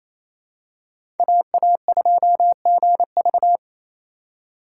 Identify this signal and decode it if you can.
Morse code, heard as a keyed tone, about 28 wpm, 700 Hz. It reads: AA2GV
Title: USA Call Signs - 4x Character Spacing - 28wpm